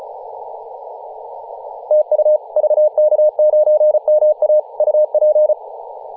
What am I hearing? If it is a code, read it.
Morse code, heard as a keyed tone, about 34 wpm, 610 Hz. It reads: TU VK9MA UP
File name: posts / 2017-nov-04-2233z-vk9ma-21-025mhz-cw